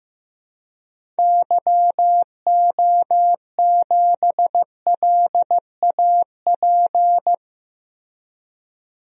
Morse code, heard as a keyed tone, about 15 wpm, 700 Hz. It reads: YO7LAP
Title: DXCC-100 Call Signs - 6 Characters - Repeated 3x - 15wpm